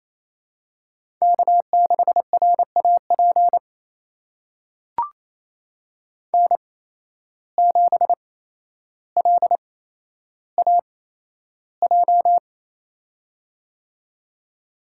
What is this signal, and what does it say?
Morse code, heard as a keyed tone, about 29 wpm, spaced as for 12 wpm, 700 Hz. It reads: K6RAP N7LAJ
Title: USA Call Signs - 5 Characters - 8x Character Spacing - 28wpm